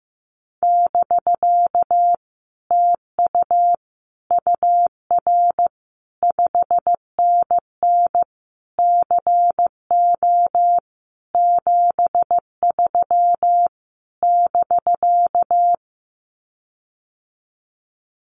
Morse code, heard as a keyed tone, about 15 wpm, 700 Hz. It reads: <BK> TU UR 5NN CO 73 <BK>
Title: Common QSO Phrases Encore 15wpm